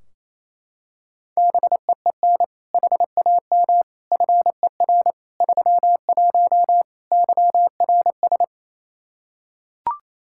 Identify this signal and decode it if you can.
Morse code, heard as a keyed tone, about 27 wpm, 700 Hz. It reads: BEEN HAM FER 31 YRS
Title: K1USN SST QSO Practice 28wpm